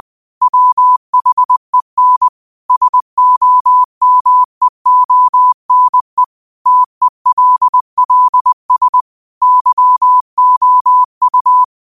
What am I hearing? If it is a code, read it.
Morse code, heard as a keyed tone, about 20 wpm, 1000 Hz. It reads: WHEN SOMEONE TELLS YOU